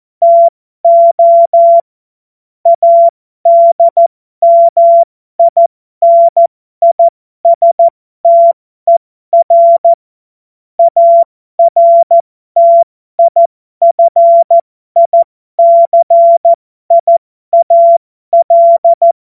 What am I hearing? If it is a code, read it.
Morse code, heard as a keyed tone, about 14 wpm, spaced as for 12 wpm, 680 Hz. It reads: TO ADMINISTER ARTIFICIAL